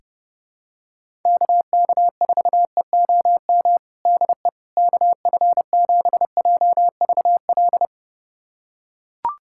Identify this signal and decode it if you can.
Morse code, heard as a keyed tone, about 31 wpm, 700 Hz. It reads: KK4EOM DE KF7JVL
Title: USA Callsign DE Callsign - 6 Characters - 30wpm